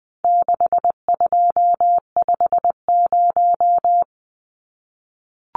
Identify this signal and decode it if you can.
Morse code, heard as a keyed tone, about 20 wpm, 700 Hz. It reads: 6250
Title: Numbers - 4-Digit 20wpm